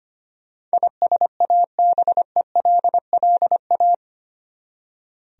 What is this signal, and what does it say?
Morse code, heard as a keyed tone, about 25 wpm, 700 Hz. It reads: ISABELLA
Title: USA Common Names - 25wpm